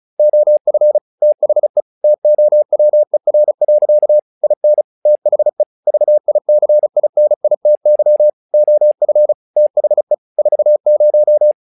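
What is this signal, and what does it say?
Morse code, heard as a keyed tone, about 36 wpm, 600 Hz. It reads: OF THE TOWER. IN THE VICINITY OF THE 40
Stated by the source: Morse practice text "Tower Trouble" sent at 35 WPM, from the SKCC - Straight Key Century Club learning center audio files.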